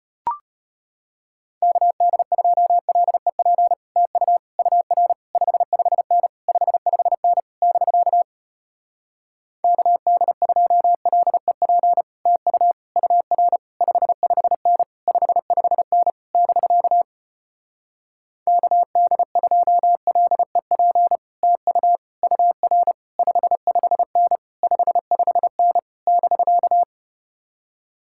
Morse code, heard as a keyed tone, about 33 wpm, 700 Hz. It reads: KD2LEP TU UR 55N 55N <BK> KD2LEP TU UR 55N 55N <BK> KD2LEP TU UR 55N 55N <BK>